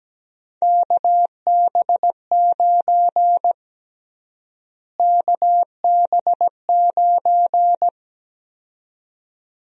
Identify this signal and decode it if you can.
Morse code, heard as a keyed tone, about 17 wpm, 700 Hz. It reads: KB9 KB9
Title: USA Call Sign Prefixes - Repeated 2x - 17wpm